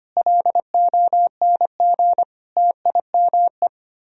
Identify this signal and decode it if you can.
Morse code, heard as a keyed tone, about 32 wpm, 700 Hz. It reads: LONG TIME